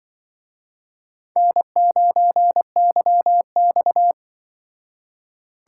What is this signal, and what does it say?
Morse code, heard as a keyed tone, about 25 wpm, 700 Hz. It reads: N9YX
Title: Speed Racing - USA Call Signs 20wpm